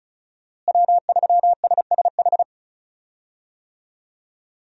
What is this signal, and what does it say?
Morse code, heard as a keyed tone, about 36 wpm, 700 Hz. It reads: W3SSH
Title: USA Call Signs - 5 Characters - Repeated 2x - 35wpm